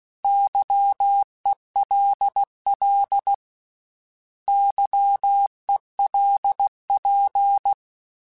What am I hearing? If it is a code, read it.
Morse code, heard as a keyed tone, about 16 wpm, 800 Hz. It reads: YELL YELP
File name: Misc / Misc-mp3s / Misc @15